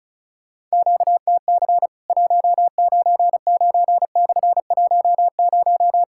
Morse code, heard as a keyed tone, about 34 wpm, 700 Hz. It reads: QTC 199/10
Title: Worked All Europe DX Contest - QTC Exchange - 35wpm